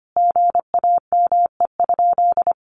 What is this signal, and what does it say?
Morse code, heard as a keyed tone, about 25 wpm, 700 Hz. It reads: GAME?
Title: Sentences from Top 1400 Words 25wpm